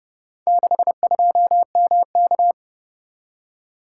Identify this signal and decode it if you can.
Morse code, heard as a keyed tone, about 29 wpm, 700 Hz. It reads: 62MK